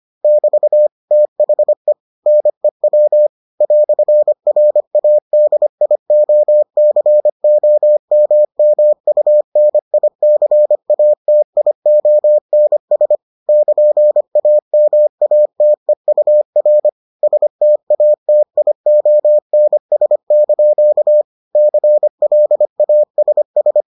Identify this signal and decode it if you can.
Morse code, heard as a keyed tone, about 25 wpm, 600 Hz. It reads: <BT> THE NEW "RADIOCOMMUNICATIONS <KN>AMATEUR STATIONS) CLASS